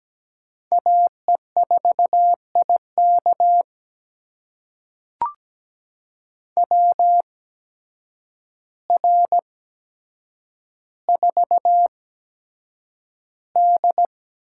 Morse code, heard as a keyed tone, about 17 wpm, spaced as for 8 wpm, 700 Hz. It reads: AE4IK WR4D
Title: USA Call Signs - 8x Character Spacing - 17wpm